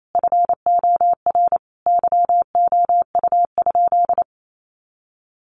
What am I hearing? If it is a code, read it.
Morse code, heard as a keyed tone, about 28 wpm, 700 Hz. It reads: FOR YOU?
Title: Sentences from Top 1400 Words 28wpm